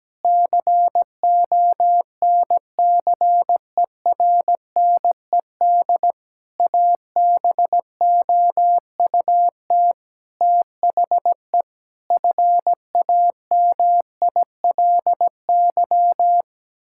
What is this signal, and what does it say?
Morse code, heard as a keyed tone, about 17 wpm, 700 Hz. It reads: CONCERNED ABOUT THE FAMILY
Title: Sentences from Top 1000 Words - Review Part 6 - 17wpm